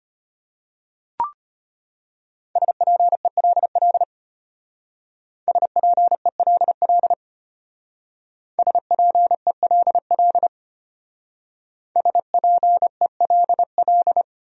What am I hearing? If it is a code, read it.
Morse code, heard as a keyed tone, about 32 wpm, 700 Hz. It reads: SPELL SPELL SPELL SPELL